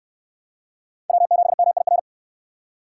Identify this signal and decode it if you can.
Morse code, heard as a keyed tone, about 62 wpm, 700 Hz. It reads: W7MIR